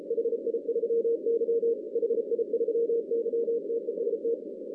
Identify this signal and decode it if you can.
Morse code, heard as a keyed tone, about 35 wpm, 470 Hz. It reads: HI3Y HI3Y TEST
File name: 2016 arrl dx cw / 2016-ARRL-DX-CW-K5TR- - 0-41